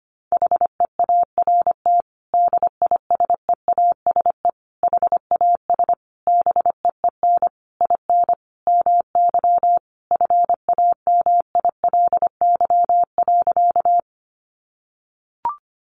Morse code, heard as a keyed tone, about 25 wpm, 700 Hz. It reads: HEART DISEASE HAS BEEN IN MY FAMILY.
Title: Sentences from Top 1200 Words 25wpm